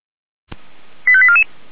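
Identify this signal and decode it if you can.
Five-tone selective call (ZVEI), heard as quick runs of five tones, tones of about 70 ms. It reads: 85844
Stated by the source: Received on SDR.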